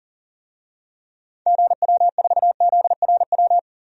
Morse code, heard as a keyed tone, about 40 wpm, 700 Hz. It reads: GW4ZRW
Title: DXCC-100 Call Signs - 6 Characters - 40wpm